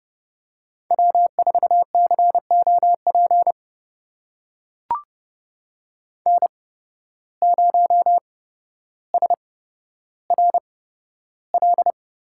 Morse code, heard as a keyed tone, about 30 wpm, spaced as for 13 wpm, 700 Hz. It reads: W4COP N0SRL
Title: USA Call Signs - 5 Characters - 8x Character Spacing - 30wpm